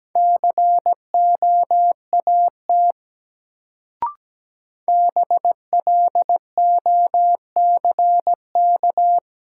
Morse code, heard as a keyed tone, about 17 wpm, 700 Hz. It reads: COAT BLOCK
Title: Sentences from Top 1600 Words - New Word Drill - 17wpm